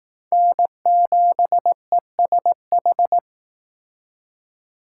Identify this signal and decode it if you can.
Morse code, heard as a keyed tone, about 18 wpm, 700 Hz. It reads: N7ESH
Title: Speed Racing - USA Call Signs 15wpm